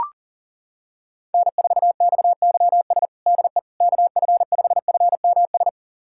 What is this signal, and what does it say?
Morse code, heard as a keyed tone, about 40 wpm, 700 Hz. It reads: N4XYS DE KF5FMS
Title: USA Callsign DE Callsign 40wpm